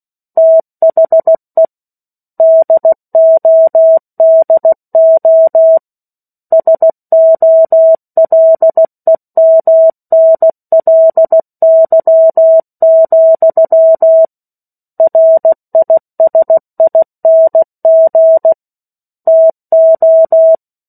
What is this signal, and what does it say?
Morse code, heard as a keyed tone, about 16 wpm, 650 Hz. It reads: THE DODO SOLEMNLY, RISING TO
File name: 03-Chapter-03 16 WPM